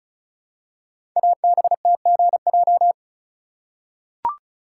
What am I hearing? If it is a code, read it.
Morse code, heard as a keyed tone, about 36 wpm, 700 Hz. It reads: ABTGJ
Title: Code Groups - 5 Letters - 35wpm